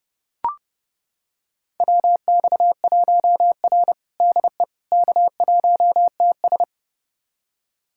Morse code, heard as a keyed tone, about 30 wpm, 700 Hz. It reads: WX1R DE K1TS
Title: USA Callsign DE Callsign - 4 Characters - 30wpm